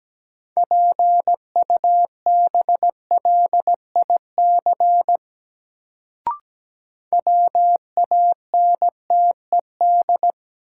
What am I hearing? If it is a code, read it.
Morse code, heard as a keyed tone, about 17 wpm, 700 Hz. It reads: PUBLIC WANTED
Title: Sets of 2 Words - New Word Drill - 17wpm